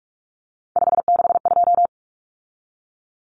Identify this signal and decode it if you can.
Morse code, heard as a keyed tone, about 44 wpm, 700 Hz. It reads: 562
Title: Numbers - 3-Digit 45wpm